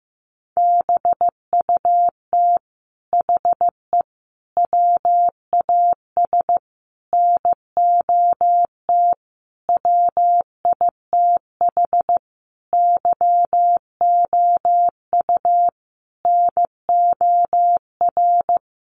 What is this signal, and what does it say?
Morse code, heard as a keyed tone, about 15 wpm, 700 Hz. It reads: BUT HE WAS NOT WITH YOU NOR
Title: Sentences from Top 1500 Words 15wpm